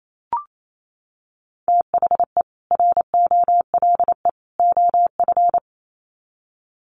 Morse code, heard as a keyed tone, about 28 wpm, 700 Hz. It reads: THE ROLE OF
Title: Sets of 3 Words 28wpm